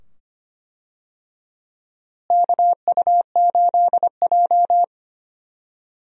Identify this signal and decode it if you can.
Morse code, heard as a keyed tone, about 25 wpm, 700 Hz. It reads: KU8J
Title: USA Call Signs - 4 Characters - Repeated 2x - 25wpm